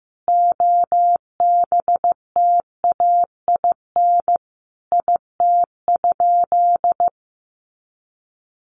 Morse code, heard as a keyed tone, about 14 wpm, 700 Hz. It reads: OBTAIN IT?
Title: Sentences from Top 1300 Words 15wpm